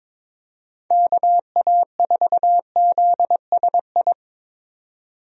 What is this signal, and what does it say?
Morse code, heard as a keyed tone, about 22 wpm, 700 Hz. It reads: KA4ZSI
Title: USA Call Signs - 6 Characters - 22wpm